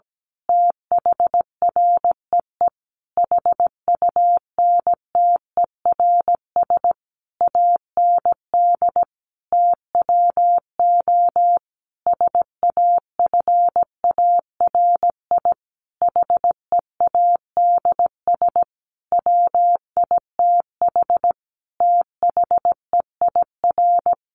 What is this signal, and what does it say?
Morse code, heard as a keyed tone, about 17 wpm, 700 Hz. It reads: THREE HUNTERS AND TWO SAFARI HEADS WITH THEIR
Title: A Sound Of Thunder 17wpm